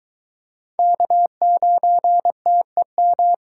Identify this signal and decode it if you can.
Morse code, heard as a keyed tone, about 22 wpm, 700 Hz. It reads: K9TEM